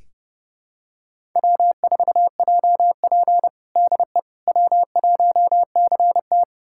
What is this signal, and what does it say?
Morse code, heard as a keyed tone, about 30 wpm, 700 Hz. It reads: W4JP DE W1CT
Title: USA Callsign DE Callsign - 4 Characters - 30wpm